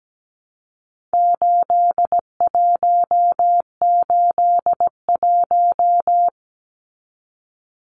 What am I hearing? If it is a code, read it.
Morse code, heard as a keyed tone, about 17 wpm, 700 Hz. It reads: 8181
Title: Numbers - 4-Digit 17wpm